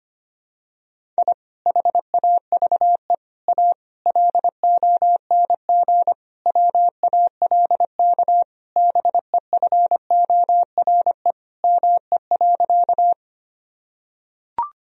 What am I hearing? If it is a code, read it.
Morse code, heard as a keyed tone, about 25 wpm, 700 Hz. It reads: I HAVE A LONG WALK BEFORE ME.
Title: Sentences from Top 1000 Words - Review Part 1 - 25wpm